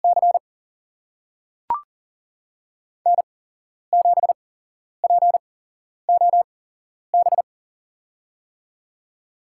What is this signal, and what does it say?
Morse code, heard as a keyed tone, about 40 wpm, spaced as for 11 wpm, 700 Hz. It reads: C N7POB
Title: USA Call Signs - 8x Character Spacing - 40wpm